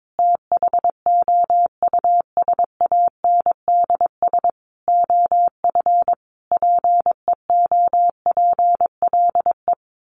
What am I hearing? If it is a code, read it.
Morse code, heard as a keyed tone, about 22 wpm, 700 Hz. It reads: THOUSANDS OF PEOPLE
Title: Sets of 5 Words Encore2 22wpm